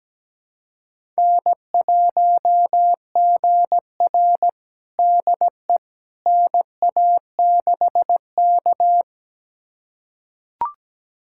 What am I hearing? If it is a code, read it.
Morse code, heard as a keyed tone, about 17 wpm, 700 Hz. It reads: N1GR DE NA6K
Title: USA Callsign DE Callsign - 4 Characters - 17wpm